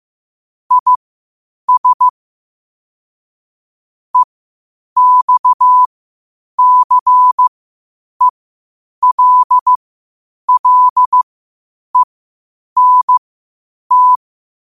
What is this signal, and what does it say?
Morse code, heard as a keyed tone, about 15 wpm, spaced as for 8 wpm, 1000 Hz. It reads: IS EXCELLENT
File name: qotdfiles / 8wpm